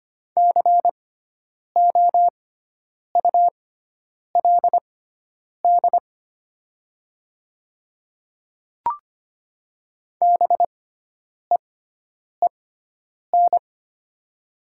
Morse code, heard as a keyed tone, about 25 wpm, spaced as for 9 wpm, 700 Hz. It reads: COULD BEEN
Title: 6x Character Spacing - Top 100 Words - 25wpm